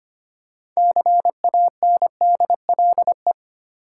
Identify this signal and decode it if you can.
Morse code, heard as a keyed tone, about 25 wpm, 690 Hz. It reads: CANDLE
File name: practice / 25wpm / Easy Words with Hints - 25wpm